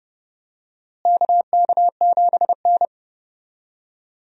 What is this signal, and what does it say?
Morse code, heard as a keyed tone, about 30 wpm, 700 Hz. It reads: KK7N
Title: USA Call Signs - 4 Characters - Repeated 3x - 30wpm